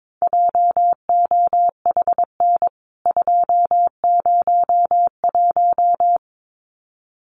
Morse code, heard as a keyed tone, about 22 wpm, 700 Hz. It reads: JOHN 201
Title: CWT Exchange 22wpm